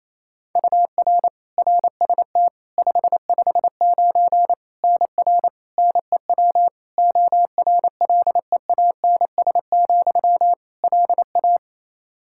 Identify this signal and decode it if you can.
Morse code, heard as a keyed tone, about 28 wpm, 700 Hz. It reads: UR RST 559 NR NEW ORLEANS, LA
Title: K1USN SST QSO Practice 28wpm